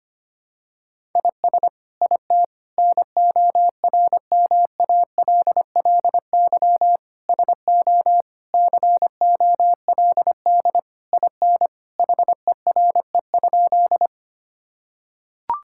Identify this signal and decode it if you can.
Morse code, heard as a keyed tone, about 25 wpm, 700 Hz. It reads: IS IT NORMALLY SO COLD IN HERE?
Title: Sentences from Top 1000 Words - Review Part 6 - 25wpm